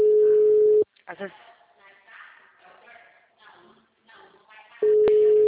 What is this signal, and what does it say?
Ringback tone (Europe, 425 Hz), with speech.